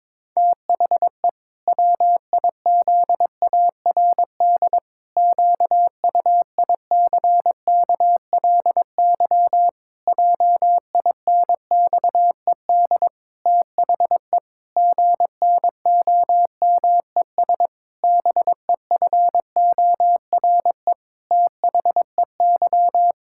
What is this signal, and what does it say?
Morse code, heard as a keyed tone, about 22 wpm, 700 Hz. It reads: THE WIZARD QUICKLY JINXED THE GNOMES BEFORE THEY
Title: Pangram Challenge v2 - 22wpm